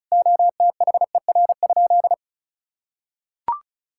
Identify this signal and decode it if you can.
Morse code, heard as a keyed tone, about 34 wpm, 700 Hz. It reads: OTHER?
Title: Sentences from Top 1000 Words - Review Part 6 - 35wpm